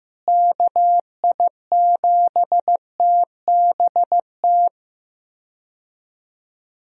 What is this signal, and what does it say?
Morse code, heard as a keyed tone, about 15 wpm, 700 Hz. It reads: KI7TBT
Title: USA Call Signs - 6 Characters - Repeated 3x - 15wpm